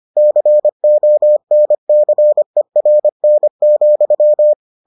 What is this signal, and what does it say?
Morse code, heard as a keyed tone, about 25 wpm, 600 Hz. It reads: CONCERN,